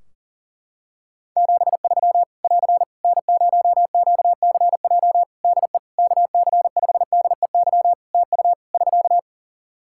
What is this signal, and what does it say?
Morse code, heard as a keyed tone, about 40 wpm, 700 Hz. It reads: 73 <AR> N0QCJ DE KC5DEY TU <SK>
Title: Practice QSO - Exchange 4 - Ending - 40wpm